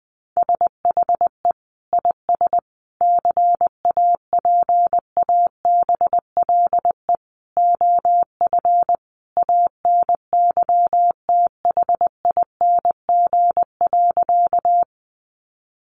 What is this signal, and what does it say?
Morse code, heard as a keyed tone, about 20 wpm, 700 Hz. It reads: SHE IS CAPABLE OF ANYTHING.
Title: Sentences from Top 1200 Words 20wpm